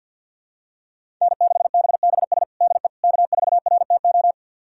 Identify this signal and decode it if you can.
Morse code, heard as a keyed tone, about 50 wpm, 700 Hz. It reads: N6BBS DE K4RTO
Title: USA Callsign DE Callsign - 5 Characters - 50wpm